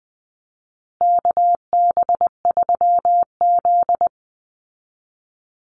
Morse code, heard as a keyed tone, about 20 wpm, 700 Hz. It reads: KB3Z
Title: Call Signs Repeated 2x - 20wpm